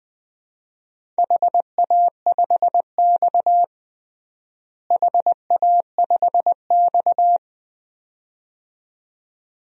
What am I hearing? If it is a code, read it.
Morse code, heard as a keyed tone, about 21 wpm, 700 Hz. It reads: HA5X HA5X
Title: DXCC-100 Call Signs - 4 Characters - Repeated 2x - 20wpm